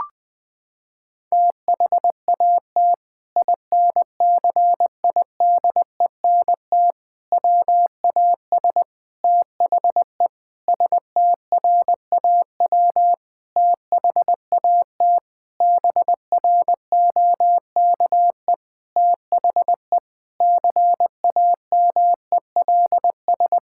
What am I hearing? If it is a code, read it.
Morse code, heard as a keyed tone, about 20 wpm, 700 Hz. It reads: THAT INCIDENT WAS THE STRAW THAT BROKE THE CAMELS
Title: Sentences with 8-Word Phrases - 20wpm